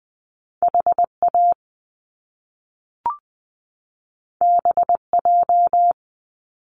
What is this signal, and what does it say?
Morse code, heard as a keyed tone, about 20 wpm, 700 Hz. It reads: H A B J
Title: Top Two Letter Combinations in Words 20wpm